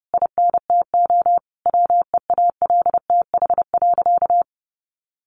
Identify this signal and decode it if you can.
Morse code, heard as a keyed tone, about 30 wpm, 700 Hz. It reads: INTO WEALTH.
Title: Sentences from Top 1400 Words 30wpm